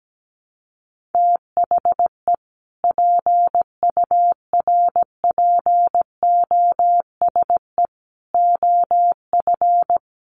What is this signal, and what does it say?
Morse code, heard as a keyed tone, about 17 wpm, 700 Hz. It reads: THE PURPOSE OF
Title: Sets of 5 Words 17wpm